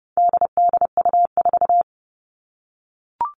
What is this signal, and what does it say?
Morse code, heard as a keyed tone, about 30 wpm, 700 Hz. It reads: DDU4
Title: All Cut Numbers - 4-Digit 30wpm